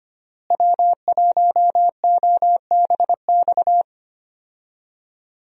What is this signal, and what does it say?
Morse code, heard as a keyed tone, about 25 wpm, 700 Hz. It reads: W1OBX
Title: USA Call Signs - 5 Characters - Repeated 3x - 25wpm